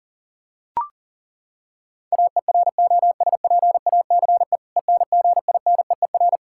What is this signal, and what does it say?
Morse code, heard as a keyed tone, about 40 wpm, 700 Hz. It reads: AEROSPACE ENGINEER
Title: Practice QSO - Exchange 3 - More Info - 40wpm